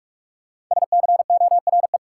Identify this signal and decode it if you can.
Morse code, heard as a keyed tone, about 46 wpm, 700 Hz. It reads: SCORE
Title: Sentences from Top 1600 Words - New Word Drill - 45wpm